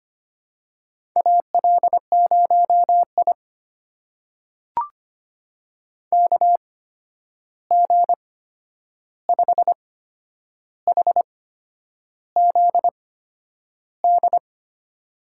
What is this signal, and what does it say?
Morse code, heard as a keyed tone, about 25 wpm, spaced as for 9 wpm, 700 Hz. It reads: AL0I KG5HZD
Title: USA Call Signs - 8x Character Spacing - 25wpm